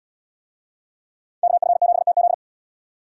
Morse code, heard as a keyed tone, about 68 wpm, 700 Hz. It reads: KF7TZ